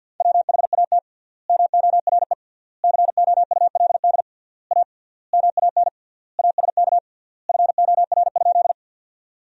Morse code, heard as a keyed tone, about 50 wpm, 700 Hz. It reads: WHAT MORE COULD A MAN ASK FOR?
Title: 3x Word-Spacing - Sentences from Top 200 Words - 50wpm